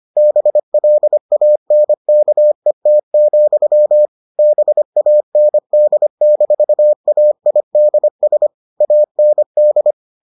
Morse code, heard as a keyed tone, about 25 wpm, 600 Hz. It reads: BLANKET, BAND-AIDS AND